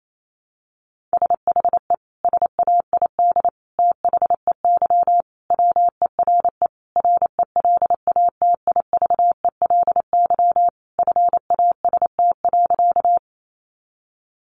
Morse code, heard as a keyed tone, about 28 wpm, 700 Hz. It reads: SHE SAID THEY WERE RELATIVELY FAST.
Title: Sentences from Top 1100 Words 28wpm